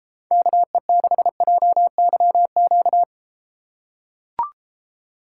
Morse code, heard as a keyed tone, about 33 wpm, 700 Hz. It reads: KE6JYQ